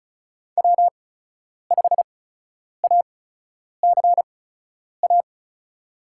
Morse code, heard as a keyed tone, about 34 wpm, spaced as for 10 wpm, 700 Hz. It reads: W5ACA
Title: USA Call Signs - 5 Characters - 8x Character Spacing - 35wpm